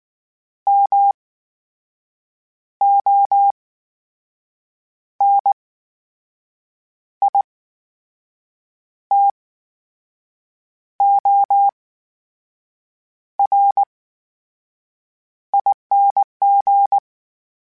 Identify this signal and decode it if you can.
Morse code, heard as a keyed tone, about 19 wpm, spaced as for 6 wpm, 800 Hz. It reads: MONITORING
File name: Special INGs / Special INGs-mp3s / VerySlowToFaster Ing @18